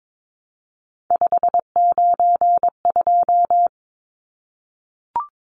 Morse code, heard as a keyed tone, about 22 wpm, 700 Hz. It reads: 592
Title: Numbers - 3-Digit 22wpm